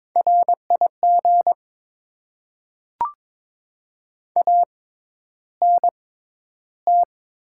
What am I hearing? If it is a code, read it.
Morse code, heard as a keyed tone, about 22 wpm, spaced as for 11 wpm, 690 Hz. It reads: RIG ANT
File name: Top 20 QSO Elements - 6x Character Spacing - 22wpm